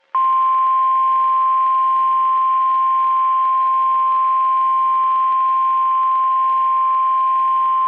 Emergency Alert System (SAME) tones (the 1050 Hz attention tone). It is heard from 0.5 to 7.2 s.
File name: eas messages / EAS WXM31 Elmira